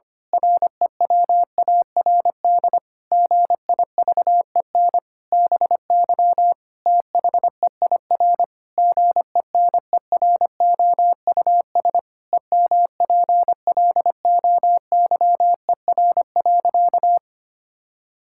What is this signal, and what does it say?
Morse code, heard as a keyed tone, about 25 wpm, 700 Hz. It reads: REWARD GIVEN BY THEIR GENEROUS EMPLOYER.
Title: Pangram Challenge v2 - 25wpm